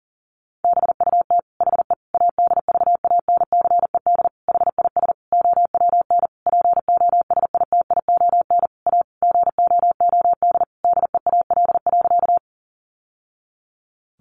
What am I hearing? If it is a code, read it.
Morse code, heard as a keyed tone, about 40 wpm, 700 Hz. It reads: BUT HE ADVANCED HIS OWN POSITION A GOOD DEAL.
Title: Sentences from Top 1200 Words 40wpm